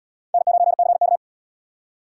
Morse code, heard as a keyed tone, about 77 wpm, 700 Hz. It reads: N9CK